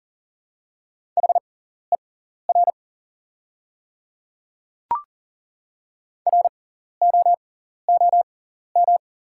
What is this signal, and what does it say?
Morse code, heard as a keyed tone, about 38 wpm, spaced as for 14 wpm, 690 Hz. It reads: HER ROOM
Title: 6x Character Spacing - Top 300 Words - 40wpm